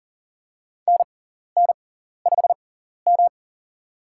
Morse code, heard as a keyed tone, about 40 wpm, spaced as for 14 wpm, 700 Hz. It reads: NN5M
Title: USA Call Signs - 4 Characters - 6x Character Spacing - 40wpm